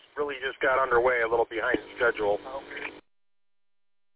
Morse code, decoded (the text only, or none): none